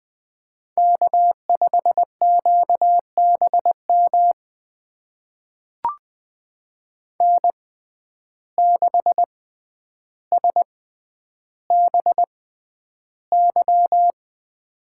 Morse code, heard as a keyed tone, about 20 wpm, spaced as for 10 wpm, 700 Hz. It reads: K5QBM N6SBY
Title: USA Call Signs - 5 Characters - 6x Character Spacing - 20wpm